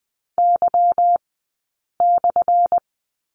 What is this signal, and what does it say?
Morse code, heard as a keyed tone, about 20 wpm, 700 Hz. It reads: Y /
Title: 177 Lesson 29d - Slash - Two Characters 20wpm